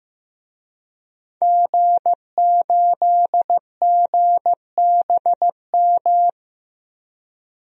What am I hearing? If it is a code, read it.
Morse code, heard as a keyed tone, about 15 wpm, 700 Hz. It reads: G8GBM
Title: DXCC-100 Call Signs - 5 Characters - Repeated 2x - 15wpm